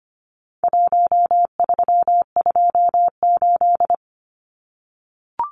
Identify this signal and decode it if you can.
Morse code, heard as a keyed tone, about 25 wpm, 700 Hz. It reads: 1328
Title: Numbers - 4-Digit 25wpm